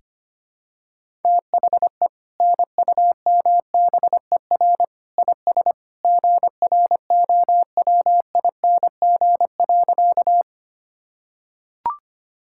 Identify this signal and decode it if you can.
Morse code, heard as a keyed tone, about 25 wpm, 700 Hz. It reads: THE NUMBER IS GROWING.
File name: practice / 25wpm / Sentences with 2-Word Phrases - 25wpm